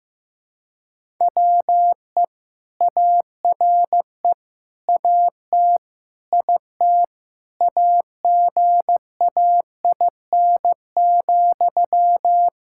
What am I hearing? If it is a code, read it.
Morse code, heard as a keyed tone, about 15 wpm, 700 Hz. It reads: WE ARE AT IT AGAIN,
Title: Sentences from Top 1000 Words - Review Part 5 - 15wpm